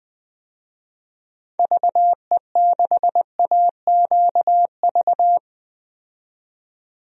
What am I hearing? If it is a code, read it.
Morse code, heard as a keyed tone, about 20 wpm, 700 Hz. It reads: VE6AQV
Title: DXCC-100 Call Signs - 6 Characters - 20wpm